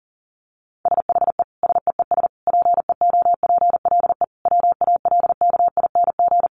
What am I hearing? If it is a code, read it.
Morse code, heard as a keyed tone, about 40 wpm, 700 Hz. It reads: SHE SEES PEOPLE WALKING
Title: Easy Phrases 40wpm